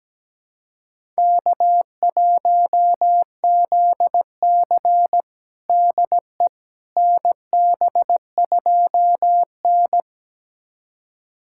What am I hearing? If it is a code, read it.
Morse code, heard as a keyed tone, about 17 wpm, 700 Hz. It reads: K1ZC DE NB2N
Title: USA Callsign DE Callsign - 4 Characters - 17wpm